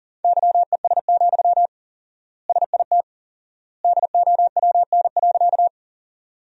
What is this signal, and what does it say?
Morse code, heard as a keyed tone, about 40 wpm, 700 Hz. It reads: YES, SIT DOWN.
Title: 4x Word-Spacing - Sentences from Top 500 Words - 40wpm